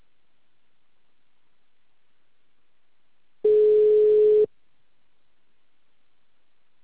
Ringback tone (Europe, 425 Hz).